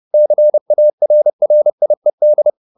Morse code, heard as a keyed tone, about 30 wpm, 600 Hz. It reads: CARRIED
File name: the old man and the sea / the old man and the sea 1 30wpm